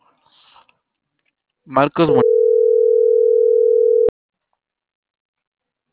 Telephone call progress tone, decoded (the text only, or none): ringback tone